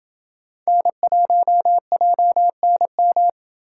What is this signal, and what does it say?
Morse code, heard as a keyed tone, about 27 wpm, 700 Hz. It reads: N1JNM